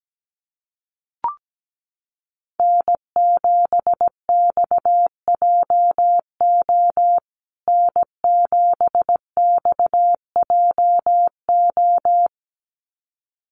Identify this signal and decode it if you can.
Morse code, heard as a keyed tone, about 17 wpm, 700 Hz. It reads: N7XJO N7XJO
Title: USA Call Signs Repeated 2x 17wpm